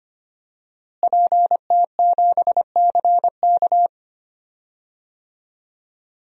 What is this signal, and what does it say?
Morse code, heard as a keyed tone, about 25 wpm, 700 Hz. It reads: PT7CK
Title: DXCC-100 Call Signs - 5 Characters - Repeated 2x - 25wpm